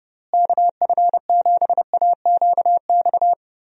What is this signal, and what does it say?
Morse code, heard as a keyed tone, about 30 wpm, 700 Hz. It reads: KF7AQX
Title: USA Call Signs - 6 Characters - Repeated 3x - 30wpm